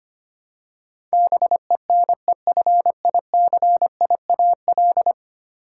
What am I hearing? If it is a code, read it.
Morse code, heard as a keyed tone, about 25 wpm, 700 Hz. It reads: BENEFICIAL
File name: Ten Letter Words - 25wpm